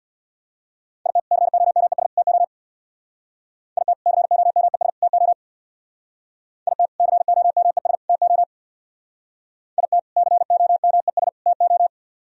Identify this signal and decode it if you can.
Morse code, heard as a keyed tone, about 60 wpm, 700 Hz. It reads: IT COMES TO IT COMES TO IT COMES TO IT COMES TO
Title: Speed-Racing - Sets of 3 Words 50wpm